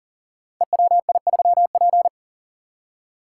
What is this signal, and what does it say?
Morse code, heard as a keyed tone, about 40 wpm, 700 Hz. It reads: EWI3P
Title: Code Groups - 5 Letter:Numbers - 40wpm